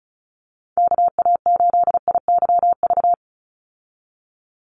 Morse code, heard as a keyed tone, about 34 wpm, 700 Hz. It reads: KA8IYV
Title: USA Call Signs 35wpm